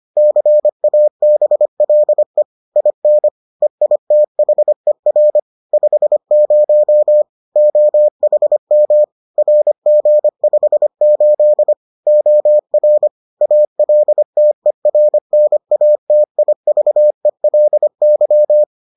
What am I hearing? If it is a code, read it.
Morse code, heard as a keyed tone, about 25 wpm, 600 Hz. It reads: CABLE IN EITHER 50 OHM RG58 OR ALTERNATIVELY